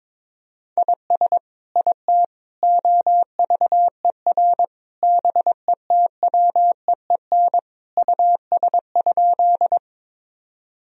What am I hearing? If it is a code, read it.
Morse code, heard as a keyed tone, about 22 wpm, 700 Hz. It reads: IS IT OVER BETWEEN US?
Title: Sentences from Top 1000 Words - Review Part 1 - 22wpm